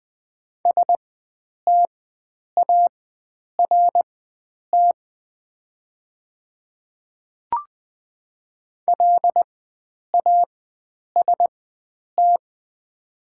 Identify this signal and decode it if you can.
Morse code, heard as a keyed tone, about 20 wpm, spaced as for 9 wpm, 690 Hz. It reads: START LAST
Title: 4x Character Spacing - Top 300 Words - 20wpm